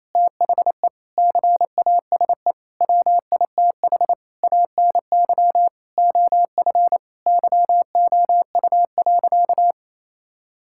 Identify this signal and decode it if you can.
Morse code, heard as a keyed tone, about 27 wpm, 700 Hz. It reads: THE CASE WITH ANY OF YOU.
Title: Sentences from Top 1600 Words 28wpm